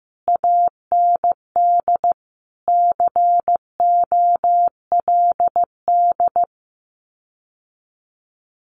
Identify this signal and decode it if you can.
Morse code, heard as a keyed tone, about 15 wpm, 700 Hz. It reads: AND COLD
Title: Common QSO Phrases Encore 15wpm